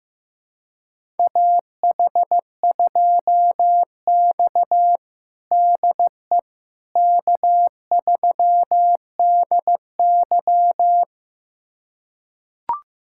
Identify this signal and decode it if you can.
Morse code, heard as a keyed tone, about 15 wpm, 700 Hz. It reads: AH2X DE K3DY
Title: USA Callsign DE Callsign - 4 Characters - 15wpm